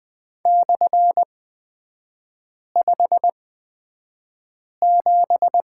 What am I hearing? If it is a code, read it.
Morse code, heard as a keyed tone, about 20 wpm, spaced as for 5 wpm, 700 Hz. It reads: /57
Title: 250 Lesson 40e - BK - Three Characters 20wpm